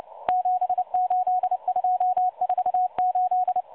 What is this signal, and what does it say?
Morse code, heard as a keyed tone, about 29 wpm, 730 Hz. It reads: 78248
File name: M12 - 8116 kHz CW 2-11-14